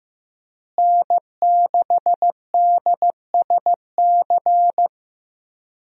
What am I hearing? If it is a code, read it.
Morse code, heard as a keyed tone, about 15 wpm, 700 Hz. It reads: N6DSC
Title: USA Call Signs - 5 Characters - Repeated 2x - 15wpm